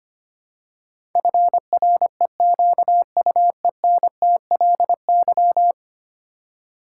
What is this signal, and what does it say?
Morse code, heard as a keyed tone, about 25 wpm, 700 Hz. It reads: FREQUENTLY
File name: Ten Letter Words - 25wpm